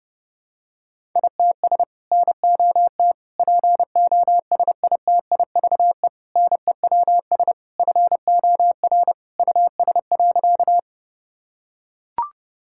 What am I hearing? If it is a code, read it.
Morse code, heard as a keyed tone, about 32 wpm, 700 Hz. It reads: ITS NOT POSITIVE NEWS FOR US.
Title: Sentences from Top 1000 Words - Review Part 5 - 30wpm